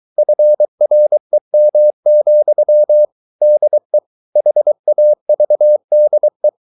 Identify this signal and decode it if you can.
Morse code, heard as a keyed tone, about 23 wpm, 600 Hz. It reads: FREM, DE HAVDE
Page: Ferie 23 wpm